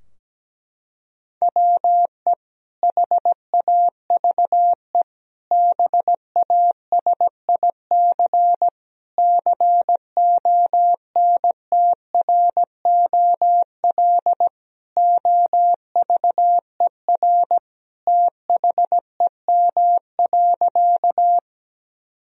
Morse code, heard as a keyed tone, about 17 wpm, 700 Hz. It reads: WE HAVE BASIC CONTROL OVER THEM.
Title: Sentences from Top 1000 Words - Review Part 3 - 17wpm